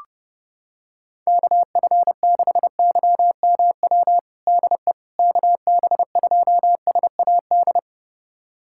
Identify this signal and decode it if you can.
Morse code, heard as a keyed tone, about 30 wpm, 700 Hz. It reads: KF6YMW DE KB2SAD
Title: USA Callsign DE Callsign 30wpm